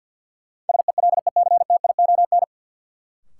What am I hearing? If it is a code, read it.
Morse code, heard as a keyed tone, about 53 wpm, 700 Hz. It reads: SELECTION
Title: Sentences from Top 1400 Words - New Word Drill Alt - 50wpm